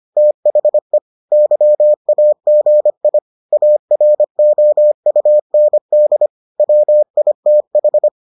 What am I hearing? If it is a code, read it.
Morse code, heard as a keyed tone, about 25 wpm, 600 Hz. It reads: THE YAGI AROUND WITH